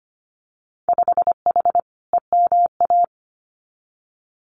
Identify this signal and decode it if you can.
Morse code, heard as a keyed tone, about 25 wpm, 690 Hz. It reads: 5H EMA
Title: Winter Field Day Exchanges Encore 25wpm